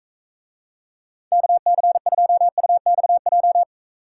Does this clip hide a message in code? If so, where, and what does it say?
Yes: Morse code from 1.3 to 3.6 s, heard as a keyed tone, about 44 wpm, 700 Hz. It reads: KC2UXJ